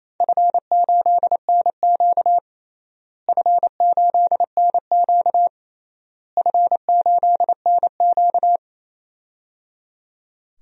Morse code, heard as a keyed tone, about 28 wpm, 700 Hz. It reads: F8NQ F8NQ F8NQ
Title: DXCC-100 Call Signs - 4 Characters - Repeated 3x - 28wpm